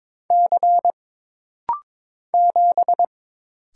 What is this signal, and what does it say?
Morse code, heard as a keyed tone, about 22 wpm, 700 Hz. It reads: C 7
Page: This rapid-fire timing is geared to push your ICR (Instant Character Recognition) skills.